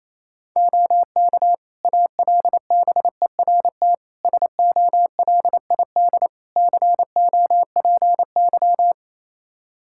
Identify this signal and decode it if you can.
Morse code, heard as a keyed tone, about 28 wpm, 700 Hz. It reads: OK ALBERT SOLID COPY
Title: Practice QSO - Exchange 3 - More Info - 28wpm